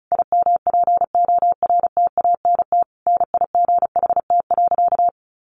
Morse code, heard as a keyed tone, about 34 wpm, 700 Hz. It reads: IMPORTANT NIGHT.
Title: Sentences from Top 1300 Words 35wpm